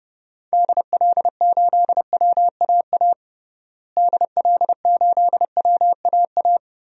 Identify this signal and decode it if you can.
Morse code, heard as a keyed tone, about 30 wpm, 700 Hz. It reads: DL8WAA DL8WAA
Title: DXCC-100 Call Signs - 6 Characters - Repeated 2x - 30wpm